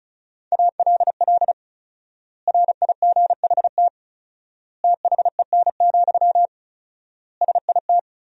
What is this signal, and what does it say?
Morse code, heard as a keyed tone, about 36 wpm, 700 Hz. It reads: ALL RIGHT THEN, SIT
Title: 4x Word-Spacing - Sentences from Top 500 Words - 35wpm